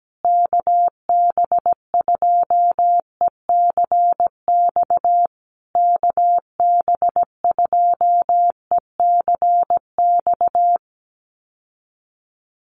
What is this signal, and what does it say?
Morse code, heard as a keyed tone, about 17 wpm, 700 Hz. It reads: KB2ECX KB2ECX
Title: USA Call Signs Repeated 3x 17wpm